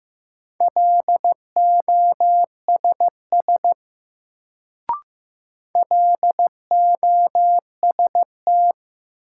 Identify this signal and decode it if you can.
Morse code, heard as a keyed tone, about 15 wpm, 700 Hz. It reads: LOSS LOST
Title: Sentences from Top 1400 Words - New Word Drill - 15wpm